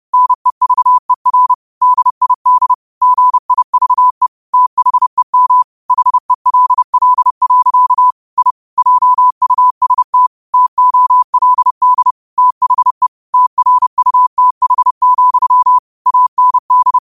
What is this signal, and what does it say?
Morse code, heard as a keyed tone, about 30 wpm, 1000 Hz. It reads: NEVER DID GIVE THEM HELL. I JUST TOLD THE TRUTH, AND